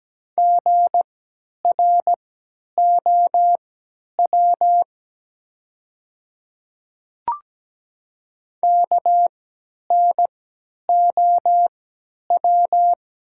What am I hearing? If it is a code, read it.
Morse code, heard as a keyed tone, about 17 wpm, spaced as for 10 wpm, 700 Hz. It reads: GROW KNOW
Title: 3x Character Spacing - Top 300 Words - 17wpm